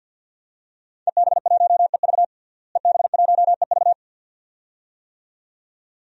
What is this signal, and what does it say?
Morse code, heard as a keyed tone, about 48 wpm, 700 Hz. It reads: EB1EV EB1EV
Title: DXCC-100 Call Signs - 5 Characters - Repeated 2x - 50wpm